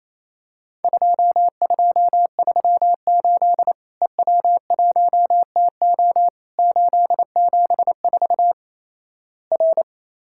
Morse code, heard as a keyed tone, about 28 wpm, 700 Hz. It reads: 2238 EW1TO 874 R
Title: Worked All Europe DX Contest - QTC Exchange - 28wpm